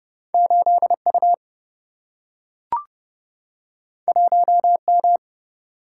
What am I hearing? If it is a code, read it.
Morse code, heard as a keyed tone, about 30 wpm, 700 Hz. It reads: 8 U 1 M
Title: Single-Letter Single-Number 30wpm